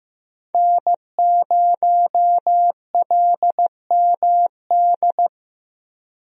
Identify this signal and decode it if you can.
Morse code, heard as a keyed tone, about 15 wpm, 700 Hz. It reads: N0LMD
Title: USA Call Signs - 5 Characters - Repeated 2x - 15wpm